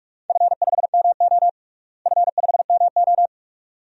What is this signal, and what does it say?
Morse code, heard as a keyed tone, about 48 wpm, 700 Hz. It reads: F5MO F5MO
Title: DXCC-100 Call Signs - 4 Characters - Repeated 2x - 45wpm